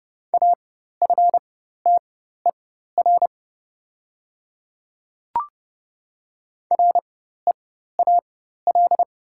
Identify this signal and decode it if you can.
Morse code, heard as a keyed tone, about 30 wpm, spaced as for 14 wpm, 690 Hz. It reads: AFTER REAL
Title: 4x Character Spacing - Top 300 Words - 30wpm